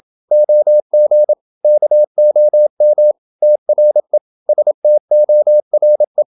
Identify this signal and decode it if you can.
Morse code, heard as a keyed tone, about 28 wpm, 600 Hz. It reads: OG KOM TRE STORE
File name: Ferie QRQ 27wpm